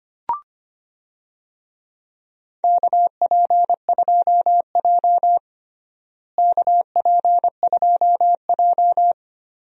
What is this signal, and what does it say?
Morse code, heard as a keyed tone, about 25 wpm, 700 Hz. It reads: KP2J KP2J
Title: DXCC-100 Call Signs - 4 Characters - Repeated 2x - 25wpm